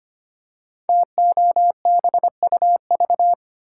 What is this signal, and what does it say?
Morse code, heard as a keyed tone, about 25 wpm, 690 Hz. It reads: TOBUV
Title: Code Groups - 5 Letters - 25wpm